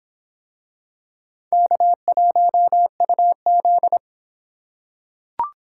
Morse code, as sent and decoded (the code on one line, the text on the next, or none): -.- .---- ..- --..
K1UZ